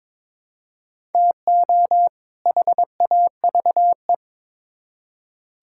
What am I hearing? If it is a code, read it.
Morse code, heard as a keyed tone, about 22 wpm, 700 Hz. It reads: TO HAVE
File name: Build A Phrase - 2-Word Phrases - 22wpm